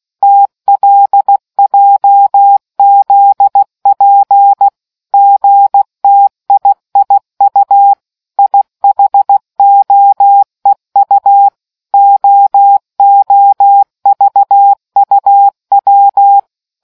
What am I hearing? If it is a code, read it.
Morse code, heard as a keyed tone, about 16 wpm, 800 Hz. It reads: TLJZP GTIIU IHOEU OOVUW